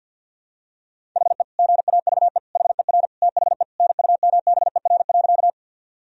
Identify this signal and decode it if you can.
Morse code, heard as a keyed tone, about 52 wpm, 700 Hz. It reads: HE GAVE HER THE NUMBER.
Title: Sentences from Top 1000 Words - Review Part 1 - 50wpm